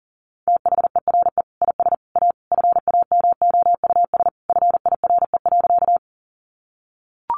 Morse code, heard as a keyed tone, about 40 wpm, 700 Hz. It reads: THERE IS A FAMOUS FIRE.
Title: Sentences from Top 1500 Words 40wpm